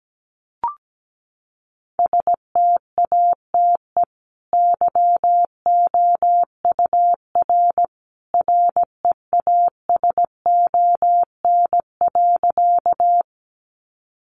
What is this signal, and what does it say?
Morse code, heard as a keyed tone, about 17 wpm, 700 Hz. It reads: STATE YOUR REASON.
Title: Sentences from Top 700 Words 17wpm